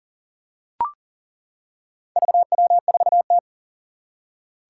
Morse code, heard as a keyed tone, about 40 wpm, 700 Hz. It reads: VW4T
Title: Four Challenging Letter-Number Combinations 40wpm